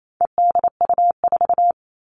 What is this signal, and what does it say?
Morse code, heard as a keyed tone, about 28 wpm, 700 Hz. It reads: EDU4
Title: All Cut Numbers - 4-Digit 28wpm